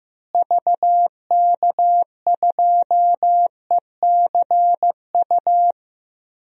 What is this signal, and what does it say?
Morse code, heard as a keyed tone, about 15 wpm, 700 Hz. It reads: VK2ECU
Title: DXCC-100 Call Signs - 6 Characters - Repeated 3x - 15wpm